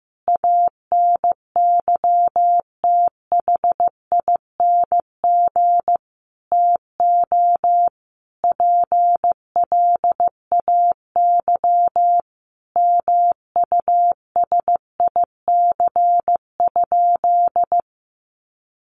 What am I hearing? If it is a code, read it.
Morse code, heard as a keyed tone, about 15 wpm, 700 Hz. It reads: ANYTHING TO PLAY MUSIC?
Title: Sentences from Top 1000 Words 15wpm